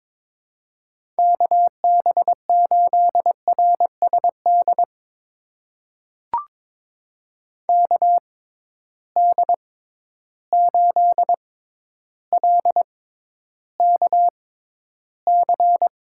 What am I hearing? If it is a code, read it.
Morse code, heard as a keyed tone, about 22 wpm, spaced as for 11 wpm, 700 Hz. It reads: KB8RSD KD8LKC
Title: USA Call Signs - 6 Characters - 6x Character Spacing - 22wpm